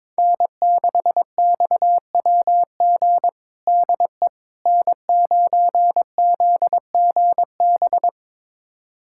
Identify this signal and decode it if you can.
Morse code, heard as a keyed tone, about 22 wpm, 700 Hz. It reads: N6XWG DE N9ZGB
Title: USA Callsign DE Callsign - 5 Characters - 22wpm